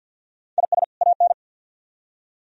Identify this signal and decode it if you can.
Morse code, heard as a keyed tone, about 65 wpm, 700 Hz. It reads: IS AN